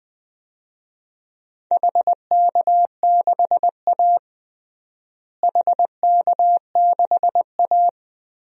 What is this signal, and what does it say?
Morse code, heard as a keyed tone, about 20 wpm, 700 Hz. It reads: HK6A HK6A
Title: DXCC-100 Call Signs - 4 Characters - Repeated 2x - 20wpm